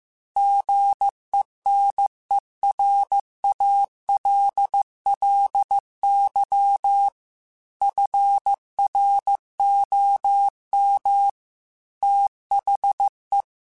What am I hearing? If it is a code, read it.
Morse code, heard as a keyed tone, about 15 wpm, 800 Hz. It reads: GENERALLY FROM THE
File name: Weather Sets / Weather Sets-mp3s / Weather 5 22 @13